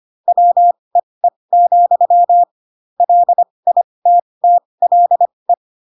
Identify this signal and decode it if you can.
Morse code, heard as a keyed tone, about 25 wpm, spaced as for 20 wpm, 700 Hz. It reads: WEE, LITTLE
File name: Pinocchio / Pinocchio ch1 15 25 ik1hge